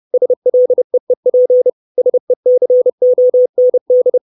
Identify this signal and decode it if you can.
Morse code, heard as a keyed tone, about 30 wpm, 500 Hz. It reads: SLEEP SECOND
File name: 632 Words 30WPM